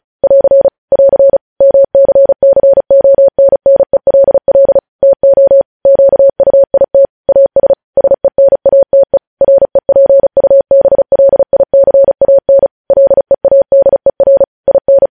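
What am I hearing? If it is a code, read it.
Morse code, heard as a keyed tone, about 36 wpm, 550 Hz. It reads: <AR> <AR> MCCONNELL TO QUIT AS SENATE REPUBLICAN LEADER IN
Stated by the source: News headlines sent in Morse code at 35 WPM.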